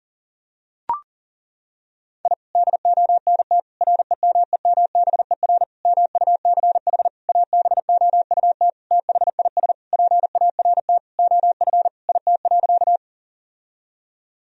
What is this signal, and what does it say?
Morse code, heard as a keyed tone, about 37 wpm, 700 Hz. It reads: I DONT REMEMBER MUCH ABOUT THIS PART OF IT.
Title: Sentences from Top 1000 Words - Review Part 1 - 40wpm